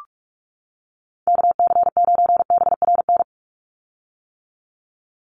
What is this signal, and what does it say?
Morse code, heard as a keyed tone, about 44 wpm, 700 Hz. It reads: KC9BRN
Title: USA Call Signs 45wpm